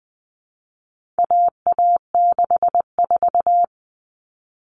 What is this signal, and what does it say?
Morse code, heard as a keyed tone, about 20 wpm, 700 Hz. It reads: AA64
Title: All Cut Numbers - 4-Digit 20wpm